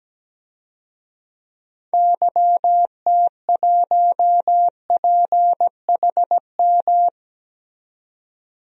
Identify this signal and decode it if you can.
Morse code, heard as a keyed tone, about 17 wpm, 700 Hz. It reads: YT1PHM
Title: DXCC-100 Call Signs - 6 Characters - 17wpm